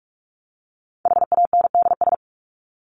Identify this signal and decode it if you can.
Morse code, heard as a keyed tone, about 44 wpm, 700 Hz. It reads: HANDS
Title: ES:S Suffix - Top 100 Words with Base Words 45wpm